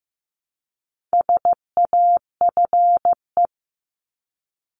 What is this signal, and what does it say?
Morse code, heard as a keyed tone, about 15 wpm, 700 Hz. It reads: SAFE
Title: Top 751-1000 Words 15wpm